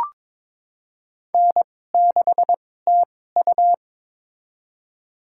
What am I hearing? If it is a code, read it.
Morse code, heard as a keyed tone, about 22 wpm, spaced as for 16 wpm, 700 Hz. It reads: N6TU
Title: USA Call Signs - 2x Character Spacing - 22wpm